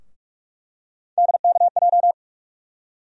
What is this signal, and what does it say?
Morse code, heard as a keyed tone, about 46 wpm, 700 Hz. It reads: DKJ
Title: USA Call Sign Postfixes - 3 Letters 45wpm